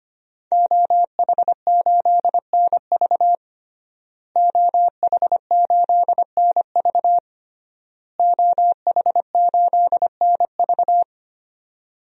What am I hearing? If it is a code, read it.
Morse code, heard as a keyed tone, about 25 wpm, 700 Hz. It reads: OH8NV OH8NV OH8NV
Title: DXCC-100 Call Signs - 5 Characters - Repeated 3x - 25wpm